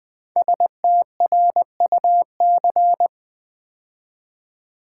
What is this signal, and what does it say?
Morse code, heard as a keyed tone, about 20 wpm, 700 Hz. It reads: STRUC
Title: Build A Word - 5 Letters At A Time - 10-Letter Words - 20wpm